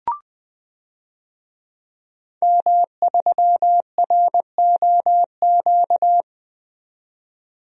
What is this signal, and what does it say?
Morse code, heard as a keyed tone, about 20 wpm, 700 Hz. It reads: M3ROQ
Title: DXCC-100 Call Signs - 5 Characters - 20wpm